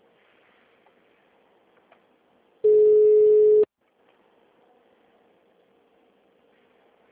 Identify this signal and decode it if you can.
Ringback tone (Europe, 425 Hz).
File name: out- -2007- - - .2873